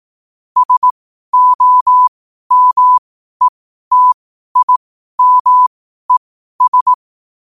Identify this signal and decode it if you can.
Morse code, heard as a keyed tone, about 18 wpm, spaced as for 12 wpm, 1000 Hz. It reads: SOMETIMES
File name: qotdfiles / 12wpm